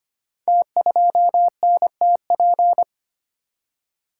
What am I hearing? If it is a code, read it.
Morse code, heard as a keyed tone, about 25 wpm, 700 Hz. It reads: T2NTP
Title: Code Groups - 5 Letter:Numbers - 25wpm